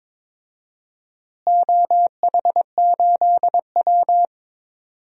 Morse code, heard as a keyed tone, about 22 wpm, 700 Hz. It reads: OH8W
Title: DXCC-100 Call Signs - 4 Characters - Repeated 2x - 22wpm